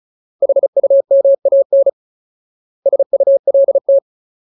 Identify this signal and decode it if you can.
Morse code, heard as a keyed tone, about 34 wpm, 550 Hz. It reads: HUMAN SULT
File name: Ultimate Word List / 35WPM Ultimate